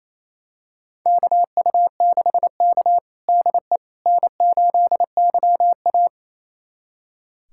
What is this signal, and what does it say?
Morse code, heard as a keyed tone, about 28 wpm, 700 Hz. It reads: KU6K DE N8YA
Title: USA Callsign DE Callsign - 4 Characters - 28wpm